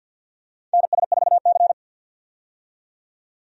Morse code, heard as a keyed tone, about 53 wpm, 700 Hz. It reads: NS4C
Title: USA Call Signs - 4 Characters - 50wpm